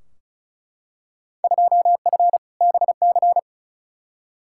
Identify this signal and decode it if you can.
Morse code, heard as a keyed tone, about 36 wpm, 700 Hz. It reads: 2F BC
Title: Field Day Exchanges Encore - 35wpm